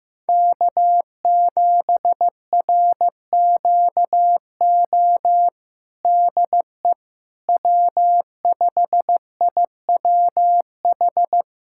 Morse code, heard as a keyed tone, about 16 wpm, 700 Hz. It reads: K7RQO DE W5IWH
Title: USA Callsign DE Callsign - 5 Characters - 15wpm